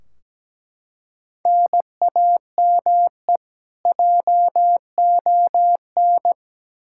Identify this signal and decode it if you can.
Morse code, heard as a keyed tone, about 17 wpm, 700 Hz. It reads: NAME JON
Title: K1USN SST QSO Practice 17wpm